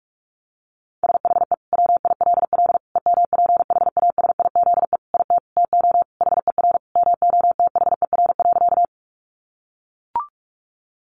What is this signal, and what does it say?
Morse code, heard as a keyed tone, about 44 wpm, 700 Hz. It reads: SHE WILL EMPHASIZE IT TO HER MOTHER.
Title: Sentences from Top 1500 Words 45wpm